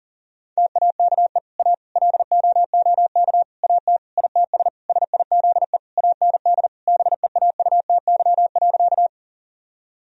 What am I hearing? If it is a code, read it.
Morse code, heard as a keyed tone, about 37 wpm, 700 Hz. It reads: TAKE A LOOK AT ITS SIZE AND BEAUTY.
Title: Sentences from Top 1000 Words - Review Part 2 - 40wpm